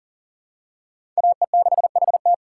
Morse code, heard as a keyed tone, about 40 wpm, 700 Hz. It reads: AE6HT
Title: USA Call Signs - 5 Characters - 2x Character Spacing - 40wpm